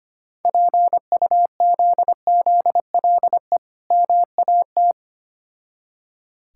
Morse code, heard as a keyed tone, about 25 wpm, 690 Hz. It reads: PUZZLE MAT
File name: practice / 25wpm / Difficult Words with Hints - 25wpm